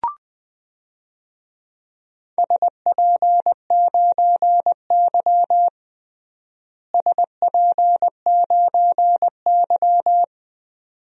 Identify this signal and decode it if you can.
Morse code, heard as a keyed tone, about 20 wpm, 700 Hz. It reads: SP9Y SP9Y
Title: DXCC-100 Call Signs - 4 Characters - Repeated 2x - 20wpm